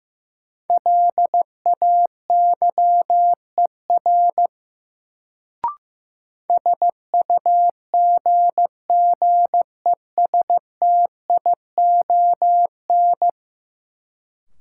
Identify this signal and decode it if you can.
Morse code, heard as a keyed tone, about 16 wpm, 700 Hz. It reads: LAYER SUGGESTION
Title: Sentences from Top 1600 Words - New Word Drill Alt - 15wpm